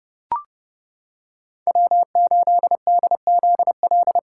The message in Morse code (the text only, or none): W8DZL